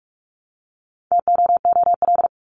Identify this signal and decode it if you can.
Morse code, heard as a keyed tone, about 44 wpm, 700 Hz. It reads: TOOL
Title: Top 751-1000 Words 45wpm